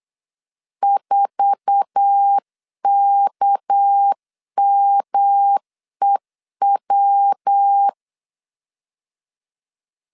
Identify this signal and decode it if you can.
Morse code, heard as a keyed tone, about 8 wpm, 800 Hz. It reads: 4KMEW
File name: CW / DET / 1 9 1500C S A 8wpm 5637